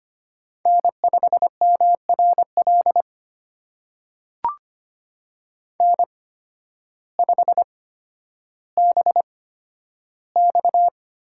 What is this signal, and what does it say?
Morse code, heard as a keyed tone, about 25 wpm, spaced as for 12 wpm, 700 Hz. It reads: N5MRL N5BX
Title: USA Call Signs - 8x Character Spacing - 25wpm